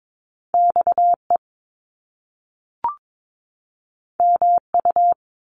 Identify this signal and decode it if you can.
Morse code, heard as a keyed tone, about 22 wpm, 700 Hz. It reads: XE MU
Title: Top Two Letter Combinations in Words 22wpm